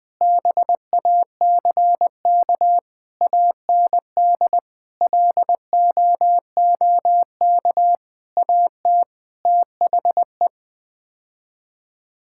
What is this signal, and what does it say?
Morse code, heard as a keyed tone, about 20 wpm, 700 Hz. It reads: BACK AND LOOK AT THE
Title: Sets of 8 Words - 20wpm